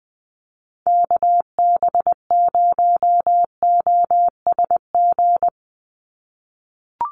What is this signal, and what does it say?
Morse code, heard as a keyed tone, about 20 wpm, 700 Hz. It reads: KB0OSG
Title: USA Call Signs 20wpm